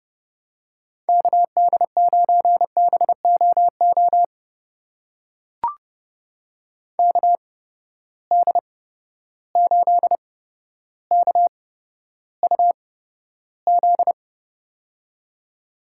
Morse code, heard as a keyed tone, about 30 wpm, spaced as for 13 wpm, 700 Hz. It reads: KD9BOO KD8KUZ
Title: USA Call Signs - 6 Characters - 8x Character Spacing - 30wpm